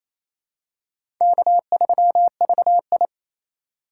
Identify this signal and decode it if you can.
Morse code, heard as a keyed tone, about 28 wpm, 700 Hz. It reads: K3VI
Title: USA Call Signs - 4 Characters - 4x Character Spacing - 28wpm